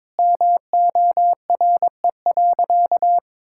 Morse code, heard as a keyed tone, about 22 wpm, 700 Hz. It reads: MORE.
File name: practice / 22wpm / Sentences with 2-Word Phrases - 22wpm